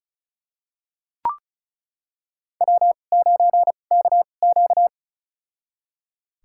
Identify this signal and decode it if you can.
Morse code, heard as a keyed tone, about 36 wpm, spaced as for 25 wpm, 700 Hz. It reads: W9KQ
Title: USA Call Signs - 2x Character Spacing - 35wpm